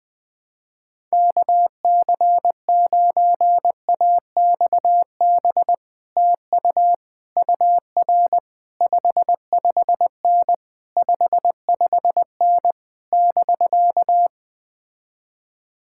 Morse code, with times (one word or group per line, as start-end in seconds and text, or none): 1.1-5.7 KC9AXB
6.2-6.9 TU
7.4-8.4 UR
8.8-10.5 55N
11.0-12.7 55N
13.1-14.3 <BK>